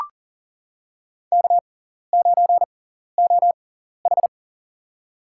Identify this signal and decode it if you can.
Morse code, heard as a keyed tone, about 38 wpm, spaced as for 14 wpm, 700 Hz. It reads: K9OH
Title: USA Call Signs - 6x Character Spacing - 40wpm